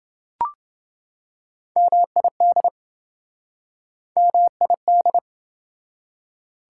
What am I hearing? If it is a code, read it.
Morse code, heard as a keyed tone, about 28 wpm, 700 Hz. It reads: MID MID